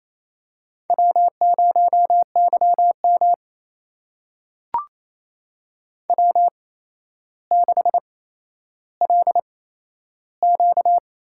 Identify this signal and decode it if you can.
Morse code, heard as a keyed tone, about 27 wpm, spaced as for 12 wpm, 700 Hz. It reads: W0YM W6LQ
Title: USA Call Signs - 4 Characters - 8x Character Spacing - 28wpm